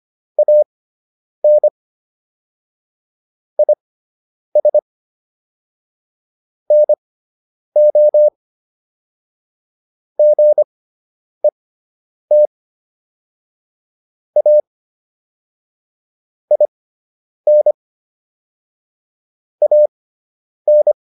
Morse code, heard as a keyed tone, about 25 wpm, spaced as for 9 wpm, 600 Hz. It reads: AN IS NO GET A IN AN